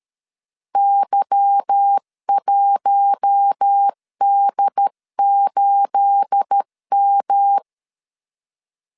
Morse code, heard as a keyed tone, about 13 wpm, 800 Hz. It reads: Y1D8M
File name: LEZIONI / 9 28 12wpm 2351